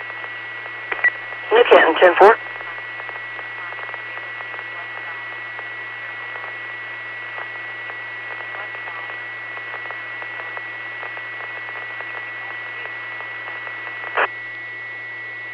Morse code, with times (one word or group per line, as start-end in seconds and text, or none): none